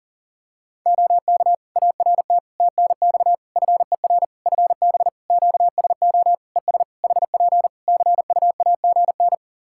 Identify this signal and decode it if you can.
Morse code, heard as a keyed tone, about 40 wpm, 700 Hz. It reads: OK ART TNX FER FB QSO ES HP CUAGN
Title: Practice QSO - Exchange 4 - Ending - 40wpm